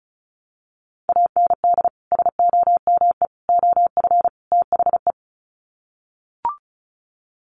Morse code, heard as a keyed tone, about 34 wpm, 700 Hz. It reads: AND SOME OF THE
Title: Sets of 4 Words 35wpm